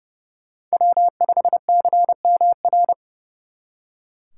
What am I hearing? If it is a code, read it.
Morse code, heard as a keyed tone, about 35 wpm, 700 Hz. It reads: W5CMR
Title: Speed Racing - USA 1x3 Call Signs 30wpm